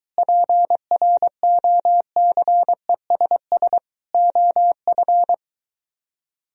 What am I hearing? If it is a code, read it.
Morse code, heard as a keyed tone, about 23 wpm, 700 Hz. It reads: PROCESS OF